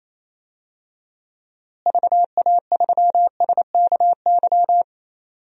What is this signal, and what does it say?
Morse code, heard as a keyed tone, about 28 wpm, 700 Hz. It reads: VA3SKY
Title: DXCC-100 Call Signs - 6 Characters - Repeated 3x - 28wpm